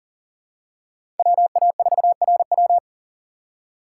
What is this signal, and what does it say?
Morse code, heard as a keyed tone, about 40 wpm, 700 Hz. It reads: WA4RW
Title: USA Call Signs - 5 Characters - 6x Character Spacing - 40wpm